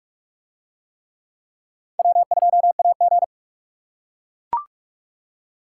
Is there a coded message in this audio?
Yes: Morse code from 2.0 to 3.2 s, heard as a keyed tone, about 44 wpm, 700 Hz. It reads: W2AG